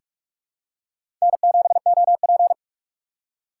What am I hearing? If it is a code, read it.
Morse code, heard as a keyed tone, about 46 wpm, 700 Hz. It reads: N7OP
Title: USA Call Signs - 4 Characters - 3x Character Spacing - 45wpm